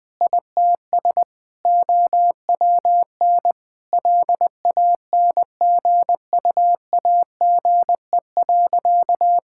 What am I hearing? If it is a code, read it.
Morse code, heard as a keyed tone, about 19 wpm, 700 Hz. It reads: ITS OWN LANGUAGE.
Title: Sentences from Top 1600 Words 20wpm